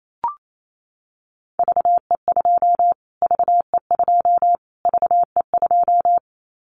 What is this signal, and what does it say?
Morse code, heard as a keyed tone, about 28 wpm, 700 Hz. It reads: VE2 VE2 VE2
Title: Callsign Prefixes for Top 25 DXCC - Repeated 3x - 28wpm